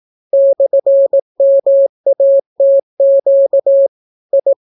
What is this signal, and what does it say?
Morse code, heard as a keyed tone, about 18 wpm, 550 Hz. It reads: /MATQ I